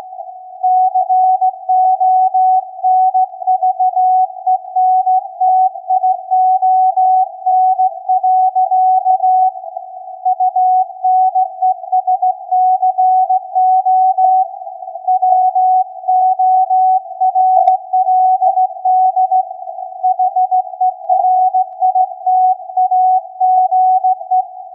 Morse code, heard as a keyed tone, about 16 wpm, 740 Hz. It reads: CONVENTION. UNESCO WORLD HERITAGE
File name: iz7slz / VLF / SAQ 16-11-2022